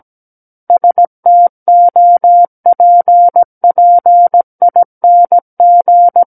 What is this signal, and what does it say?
Morse code, heard as a keyed tone, about 17 wpm, 700 Hz. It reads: STOPPING
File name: Book One / 08-Chapter 8 - 17 WPM